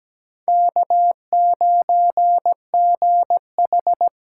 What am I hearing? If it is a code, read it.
Morse code, heard as a keyed tone, about 17 wpm, 700 Hz. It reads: K9GH
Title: USA Call Signs - 5 Characters - 8x Character Spacing - 17wpm